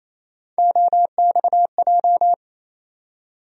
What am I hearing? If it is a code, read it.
Morse code, heard as a keyed tone, about 28 wpm, 700 Hz. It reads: OXJ
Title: USA Call Sign Postfixes - 3 Letters 28wpm